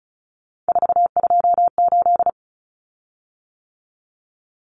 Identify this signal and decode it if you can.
Morse code, heard as a keyed tone, about 34 wpm, 700 Hz. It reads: 428
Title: Common Cut Numbers - 3-Digit 35wpm